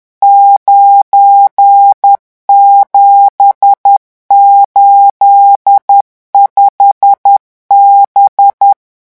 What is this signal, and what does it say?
Morse code, heard as a keyed tone, about 10 wpm, 800 Hz. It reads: 9785B
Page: REVISION 18 - 10 mots minute